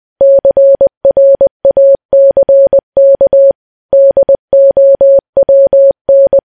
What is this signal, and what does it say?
Morse code, heard as a keyed tone, about 20 wpm, 550 Hz. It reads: CRACK DOWN
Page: news-headlines-in-morse-code-at-20-wpm